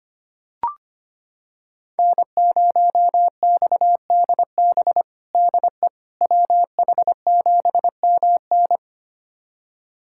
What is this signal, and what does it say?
Morse code, heard as a keyed tone, about 25 wpm, 700 Hz. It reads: N0XDB DE WH7MN
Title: USA Callsign DE Callsign - 5 Characters - 25wpm